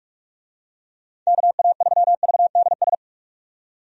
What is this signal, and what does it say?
Morse code, heard as a keyed tone, about 46 wpm, 700 Hz. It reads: KA3VDS
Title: USA Call Signs - 6 Characters - 45wpm